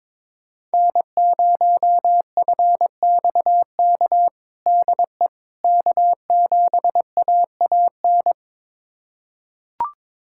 Morse code, heard as a keyed tone, about 22 wpm, 700 Hz. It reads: N0FXK DE K7AAN
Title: USA Callsign DE Callsign - 5 Characters - 22wpm